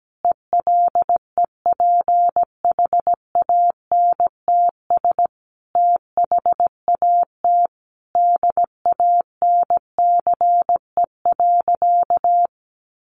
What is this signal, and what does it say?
Morse code, heard as a keyed tone, about 17 wpm, 700 Hz. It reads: ELEPHANTS THAT DANCE.
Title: Easy Phrases 17wpm